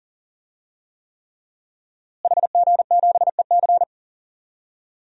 Morse code, heard as a keyed tone, about 37 wpm, 700 Hz. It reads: HG7EC
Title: DXCC-100 Call Signs - 5 Characters - Repeated 3x - 40wpm